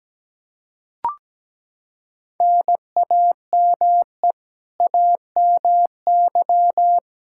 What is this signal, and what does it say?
Morse code, heard as a keyed tone, about 17 wpm, 700 Hz. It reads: NAME AMY
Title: Practice QSO - Exchange 1 - Beginning - 17wpm